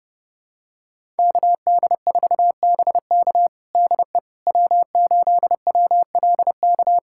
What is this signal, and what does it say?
Morse code, heard as a keyed tone, about 30 wpm, 700 Hz. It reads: KD4BK DE W8WLK
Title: USA Callsign DE Callsign 30wpm